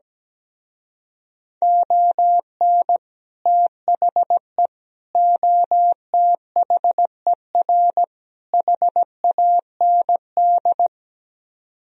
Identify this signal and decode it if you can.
Morse code, heard as a keyed tone, about 17 wpm, 700 Hz. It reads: ON THE OTHER HAND
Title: Speed-Racing - Sets of 4 Words 17wpm